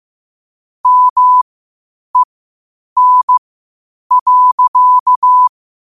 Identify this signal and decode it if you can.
Morse code, heard as a keyed tone, about 15 wpm, spaced as for 8 wpm, 1000 Hz. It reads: MEN.
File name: qotdfiles / 8wpm